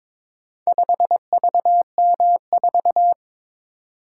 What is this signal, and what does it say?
Morse code, heard as a keyed tone, about 22 wpm, 700 Hz. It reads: 5VM4
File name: practice / 22wpm / Four Letter-Number Combinations - 22wpm